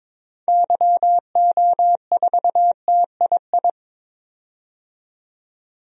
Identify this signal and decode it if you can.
Morse code, heard as a keyed tone, about 23 wpm, 700 Hz. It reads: YO4TII
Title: DXCC-100 Call Signs - 6 Characters - 22wpm